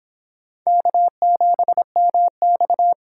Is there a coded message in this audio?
Yes: Morse code from 0.7 to 2.9 s, heard as a keyed tone, about 24 wpm, 700 Hz. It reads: K7MX